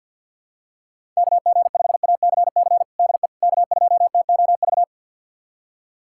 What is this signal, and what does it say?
Morse code, heard as a keyed tone, about 52 wpm, 700 Hz. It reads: KG5ACC DE K2TOV
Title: USA Callsign DE Callsign 50wpm